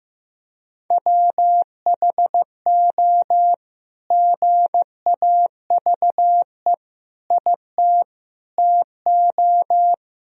Morse code, heard as a keyed tone, about 16 wpm, 700 Hz. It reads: WHO GAVE IT TO
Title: Sentences from Top 500 Words and Abbreviations 15wpm